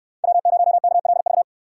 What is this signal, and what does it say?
Morse code, heard as a keyed tone, about 65 wpm, 700 Hz. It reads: W0KLV